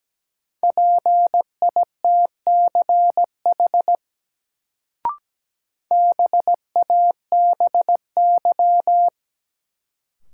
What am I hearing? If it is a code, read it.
Morse code, heard as a keyed tone, about 17 wpm, 700 Hz. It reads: PITCH BABY
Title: Sentences from Top 1300 Words - New Word Drill - 17wpm